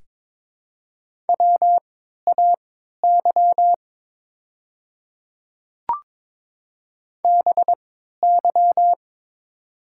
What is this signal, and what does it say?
Morse code, heard as a keyed tone, about 22 wpm, spaced as for 12 wpm, 700 Hz. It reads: WAY BY
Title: 3x Character Spacing - Top 100 Words - 22wpm